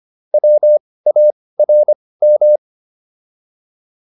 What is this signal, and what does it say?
Morse code, heard as a keyed tone, about 25 wpm, spaced as for 18 wpm, 600 Hz. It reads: WARM
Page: intermediate-practice-files